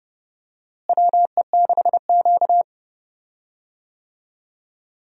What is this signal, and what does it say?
Morse code, heard as a keyed tone, about 30 wpm, 700 Hz. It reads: WE6Q
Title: USA Call Signs - 4 Characters - Repeated 2x - 30wpm